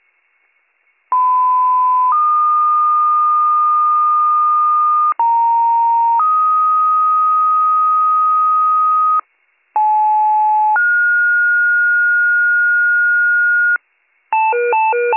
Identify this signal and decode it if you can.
Two-tone page, heard as a one-second tone, then a three-second tone, about 1010 then 1220 Hz, about 930 then 1250 Hz, about 860 then 1430 Hz. It is heard from 1.1 to 13.8 s.